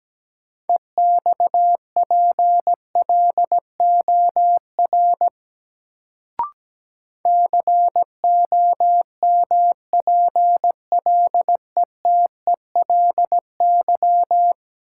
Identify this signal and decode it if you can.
Morse code, heard as a keyed tone, about 17 wpm, 700 Hz. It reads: EXPLOR COMPLETELY
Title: Sentences from Top 1100 Words - New Word Drill - 17wpm